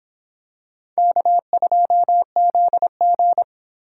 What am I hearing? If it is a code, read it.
Morse code, heard as a keyed tone, about 26 wpm, 700 Hz. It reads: K2ZG